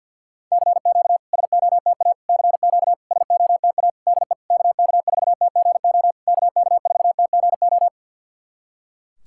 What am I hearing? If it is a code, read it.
Morse code, heard as a keyed tone, about 50 wpm, 700 Hz. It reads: CQ SOTA CQ SOTA DE KK4TGO KK4TGO
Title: SOTA QSO Practice 50wpm